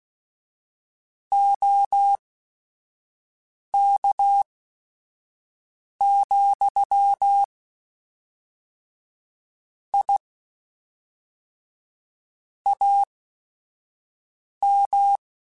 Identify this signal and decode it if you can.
Morse code, heard as a keyed tone, about 16 wpm, spaced as for 5 wpm, 800 Hz. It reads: OK, I AM
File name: QSOs / QSOs-mp3s / kc7sac@5